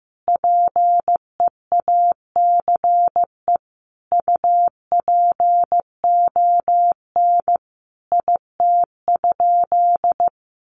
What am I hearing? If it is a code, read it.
Morse code, heard as a keyed tone, about 14 wpm, 700 Hz. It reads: PEACE UPON IT?
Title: Sentences from Top 1500 Words 15wpm